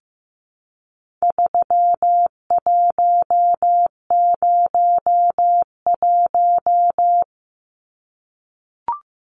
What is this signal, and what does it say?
Morse code, heard as a keyed tone, about 15 wpm, 700 Hz. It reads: 3101
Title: Numbers - 4-Digit 15wpm